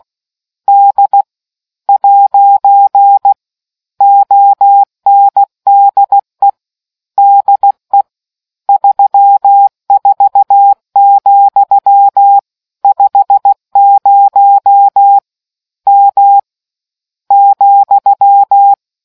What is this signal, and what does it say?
Morse code, heard as a keyed tone, about 16 wpm, 800 Hz. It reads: D ' ONDE DE 34, 50 M ,